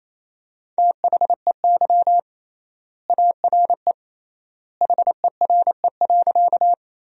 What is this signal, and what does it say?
Morse code, heard as a keyed tone, about 28 wpm, 700 Hz. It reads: THEY ARE HERE.
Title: 3x Word-Spacing - Sentences from Top 200 Words - 28wpm